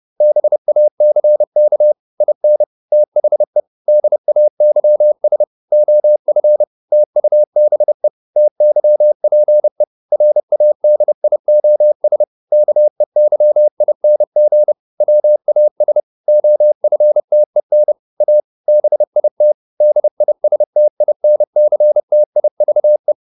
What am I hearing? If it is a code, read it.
Morse code, heard as a keyed tone, about 30 wpm, 600 Hz. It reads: BACK IN THE DAYS OF TUBE TYPE RADIOS KEYING WAS OFTEN A BIT DISTINCTIVE